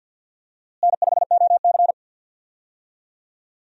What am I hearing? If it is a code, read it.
Morse code, heard as a keyed tone, about 46 wpm, 700 Hz. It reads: N5OC
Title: USA Call Signs - 4 Characters - Repeated 2x - 50wpm